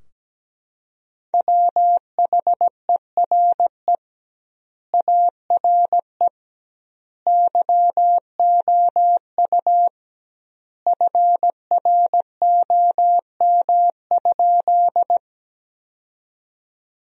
Morse code, heard as a keyed tone, about 17 wpm, 700 Hz. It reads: WHERE ARE YOU FROM?
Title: 2x Word-Spacing - Sentences from Top 200 Words - 17wpm